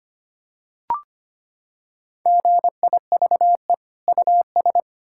Morse code, heard as a keyed tone, about 25 wpm, 700 Hz. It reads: GIVE US
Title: Sentences from Top 1000 Words - Review Part 5 - 25wpm